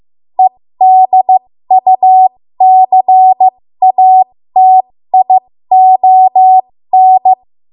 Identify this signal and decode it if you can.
Morse code, heard as a keyed tone, about 15 wpm, spaced as for 13 wpm, 750 Hz. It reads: EDUCATION
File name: Morse / 13WPM